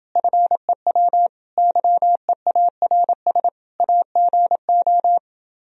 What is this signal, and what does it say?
Morse code, heard as a keyed tone, about 30 wpm, 700 Hz. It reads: FEW YEARS AGO